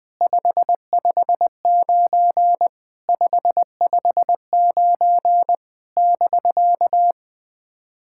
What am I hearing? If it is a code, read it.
Morse code, heard as a keyed tone, about 20 wpm, 700 Hz. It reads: 559 559 <BK>
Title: SOTA QSO Practice 20wpm